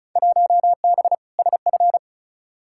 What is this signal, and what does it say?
Morse code, heard as a keyed tone, about 36 wpm, 700 Hz. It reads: 1B SF
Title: Field Day Exchanges Encore - 35wpm